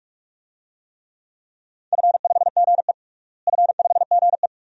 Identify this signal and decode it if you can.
Morse code, heard as a keyed tone, about 47 wpm, 700 Hz. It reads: F5GE F5GE
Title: DXCC-100 Call Signs - 4 Characters - Repeated 2x - 45wpm